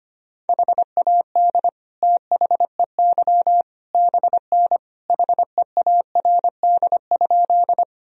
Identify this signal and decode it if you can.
Morse code, heard as a keyed tone, about 25 wpm, 700 Hz. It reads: HAD THEY BN HEARD?
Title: Sentences from Top 400 Words and Abbreviations 25wpm